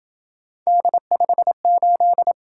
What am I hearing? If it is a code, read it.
Morse code, heard as a keyed tone, about 26 wpm, 700 Hz. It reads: D58